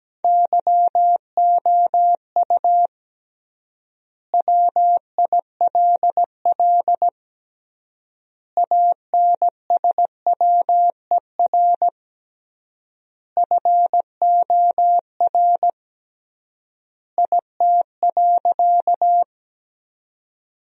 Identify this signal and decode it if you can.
Morse code, heard as a keyed tone, about 17 wpm, 700 Hz. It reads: YOU WILL ANSWER FOR IT.
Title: 3x Word-Spacing - Sentences from Top 300 Words - 17wpm